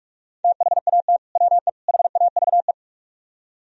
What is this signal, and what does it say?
Morse code, heard as a keyed tone, about 46 wpm, 700 Hz. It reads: THAT WE HAVE